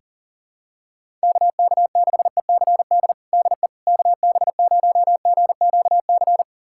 Morse code, heard as a keyed tone, about 40 wpm, 700 Hz. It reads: KK6ECD DE KB0GQC
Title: USA Callsign DE Callsign - 6 Characters - 40wpm